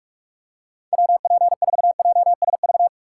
Morse code, heard as a keyed tone, about 42 wpm, 700 Hz. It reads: WP4JSV
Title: USA Call Signs - 6 Characters - 3x Character Spacing - 45wpm